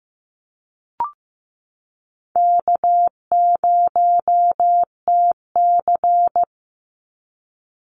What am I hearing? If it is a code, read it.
Morse code, heard as a keyed tone, about 15 wpm, 700 Hz. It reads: K0TC
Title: Easy Call Signs Repeated 3x - 15wpm